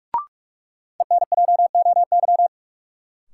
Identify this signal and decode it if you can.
Morse code, heard as a keyed tone, about 42 wpm, 700 Hz. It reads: ENJOY
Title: Sentences from Top 1300 Words - New Word Drill Alt - 45wpm